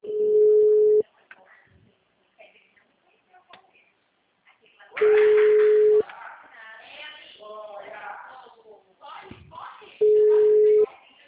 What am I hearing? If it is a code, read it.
Ringback tone (Europe, 425 Hz), with speech.
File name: out- -2038- - - .66